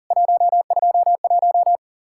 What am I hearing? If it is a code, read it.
Morse code, heard as a keyed tone, about 40 wpm, 700 Hz. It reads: 121
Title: Worked All Europe DX Contest - QTC Exchange - 40wpm